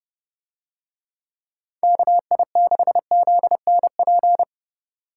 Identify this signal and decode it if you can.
Morse code, heard as a keyed tone, about 30 wpm, 700 Hz. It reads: KI6ZNP